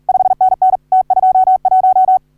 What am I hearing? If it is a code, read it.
Morse code, heard as a keyed tone, about 44 wpm, 750 Hz. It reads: 5NN T21
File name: CW / HA3OU RHOBCW REC